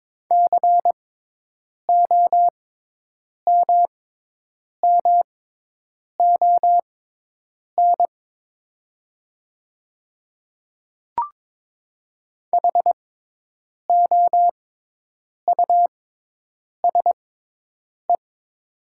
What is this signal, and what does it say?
Morse code, heard as a keyed tone, about 22 wpm, spaced as for 8 wpm, 700 Hz. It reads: COMMON HOUSE
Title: 6x Character Spacing - Top 500 Words - 22wpm